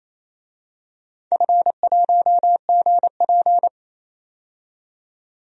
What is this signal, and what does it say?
Morse code, heard as a keyed tone, about 28 wpm, 700 Hz. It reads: F1GP
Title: DXCC-100 Call Signs - 4 Characters - 28wpm